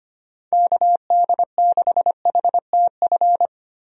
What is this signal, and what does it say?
Morse code, heard as a keyed tone, about 25 wpm, 700 Hz. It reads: KD6HTF
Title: USA Call Signs - 6 Characters - Repeated 2x - 25wpm